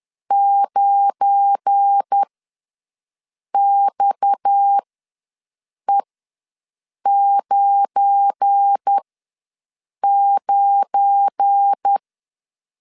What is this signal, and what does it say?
Morse code, heard as a keyed tone, about 10 wpm, spaced as for 6 wpm, 800 Hz. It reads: 9XE99
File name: LEZIONI / 7 GF9Z / 7 GF9 ws 10wpm 4034